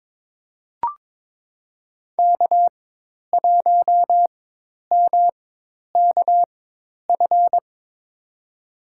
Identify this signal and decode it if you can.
Morse code, heard as a keyed tone, about 21 wpm, spaced as for 10 wpm, 700 Hz. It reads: K1MKF
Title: USA Call Signs - 4x Character Spacing - 22wpm